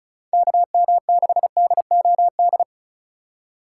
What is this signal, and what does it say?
Morse code, heard as a keyed tone, about 36 wpm, 700 Hz. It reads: KM6DOD
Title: USA Call Signs - 6 Characters - 4x Character Spacing - 35wpm